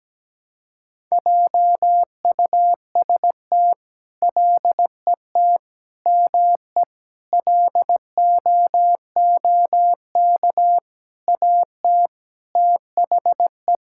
Morse code, heard as a keyed tone, about 17 wpm, 700 Hz. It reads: JUST LET ME LOOK AT THE
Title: Sentences from Top 1000 Words - Review Part 5 - 17wpm